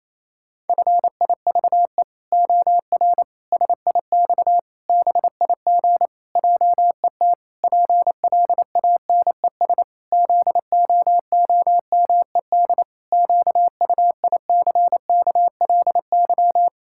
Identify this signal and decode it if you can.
Morse code, heard as a keyed tone, about 28 wpm, 700 Hz. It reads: FIVE OR SIX BIG JET PLANES ZOOMED QUICKLY
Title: Pangram Challenge v2 - 28wpm